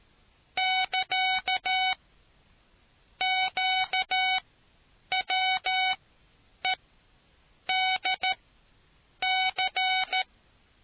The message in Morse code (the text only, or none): <KA> QWEDC